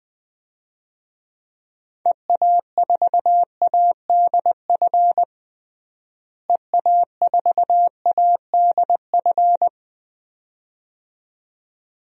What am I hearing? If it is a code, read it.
Morse code, heard as a keyed tone, about 20 wpm, 700 Hz. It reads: EA4ADF EA4ADF
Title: DXCC-100 Call Signs - 6 Characters - Repeated 2x - 20wpm